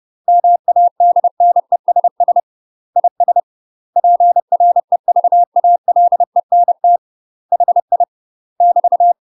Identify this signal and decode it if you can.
Morse code, heard as a keyed tone, about 30 wpm, 700 Hz. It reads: MADNESS IS PREVALENT HI <BT>
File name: UK MASTER QSOs TRAINING / 30WPM / UK MASTER QSOs 0023